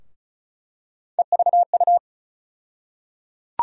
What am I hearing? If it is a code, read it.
Morse code, heard as a keyed tone, about 36 wpm, 700 Hz. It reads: EVU
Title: USA Call Sign Postfixes - 3 Letters 35wpm